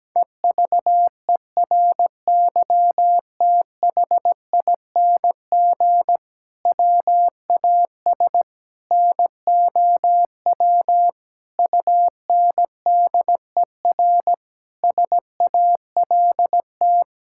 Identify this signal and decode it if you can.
Morse code, heard as a keyed tone, about 17 wpm, 700 Hz. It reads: EVERYTHING WAS NOW UNDER SALT
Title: Sentences from Top 1000 Words - Review Part 5 - 17wpm